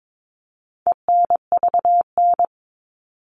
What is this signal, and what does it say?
Morse code, heard as a keyed tone, about 22 wpm, 700 Hz. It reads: ENVN
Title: All Cut Numbers - 4-Digit 22wpm